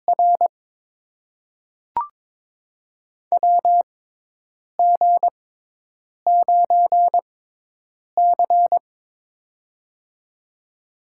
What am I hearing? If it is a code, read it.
Morse code, heard as a keyed tone, about 22 wpm, spaced as for 8 wpm, 700 Hz. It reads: R WG9C
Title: USA Call Signs - 4 Characters - 6x Character Spacing - 22wpm